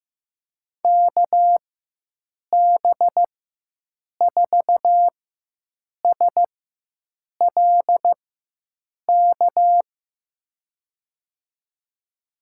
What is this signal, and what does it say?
Morse code, heard as a keyed tone, about 15 wpm, spaced as for 7 wpm, 700 Hz. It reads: KB4SLK
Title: USA Call Signs - 6 Characters - 4x Character Spacing - 15wpm